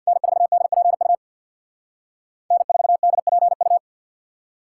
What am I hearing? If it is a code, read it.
Morse code, heard as a keyed tone, about 52 wpm, 700 Hz. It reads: N4DPU N4DPU
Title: Speed Racing - USA 1x3 Call Signs 50wpm